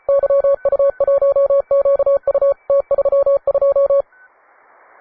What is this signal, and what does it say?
Morse code, heard as a keyed tone, about 34 wpm, 560 Hz. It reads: YU1QU T32
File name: wae / cw advanced / un4l cw